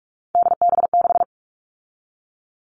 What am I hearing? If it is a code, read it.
Morse code, heard as a keyed tone, about 44 wpm, 700 Hz. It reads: DB6
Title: All Cut Numbers - 3-Digit 45wpm